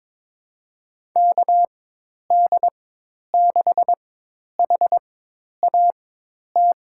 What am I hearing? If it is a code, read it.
Morse code, heard as a keyed tone, about 22 wpm, spaced as for 10 wpm, 700 Hz. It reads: KD6HAT
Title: USA Call Signs - 6 Characters - 4x Character Spacing - 22wpm